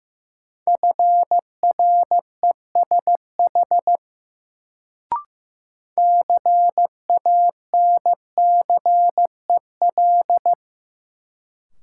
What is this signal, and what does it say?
Morse code, heard as a keyed tone, about 15 wpm, 700 Hz. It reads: FRESH CANCEL
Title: Sentences from Top 1200 Words - New Word Drill Alt - 15wpm